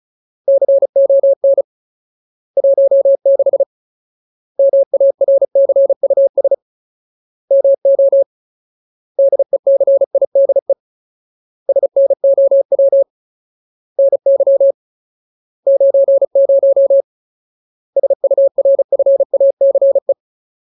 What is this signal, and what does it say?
Morse code, heard as a keyed tone, about 35 wpm, 550 Hz. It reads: CON 16 MARCUS MO DECIDE SNOW NY 90 SURFACE
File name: Ultimate Word List / 35WPM Ultimate